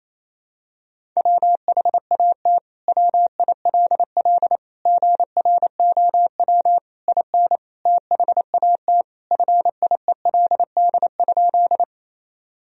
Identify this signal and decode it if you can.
Morse code, heard as a keyed tone, about 28 wpm, 700 Hz. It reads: WHAT WILL GROW IN THAT FIELD?
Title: Sentences from Top 1000 Words - Review Part 2 - 28wpm